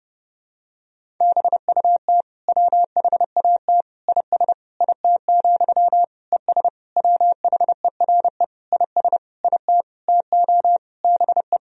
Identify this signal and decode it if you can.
Morse code, heard as a keyed tone, about 32 wpm, 700 Hz. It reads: BUT WHAT IS IT, ES WHERE IS IT TO BE
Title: Sentences from Top 300 Words and Abbreviations 30wpm